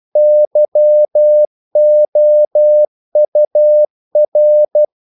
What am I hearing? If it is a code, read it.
Morse code, heard as a keyed tone, about 12 wpm, 600 Hz. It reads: YOUR
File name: CW / Practice / EnglishQuotations0312